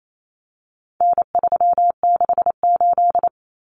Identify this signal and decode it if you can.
Morse code, heard as a keyed tone, about 28 wpm, 700 Hz. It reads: N368
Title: Common Cut Numbers - 4-Digit 28wpm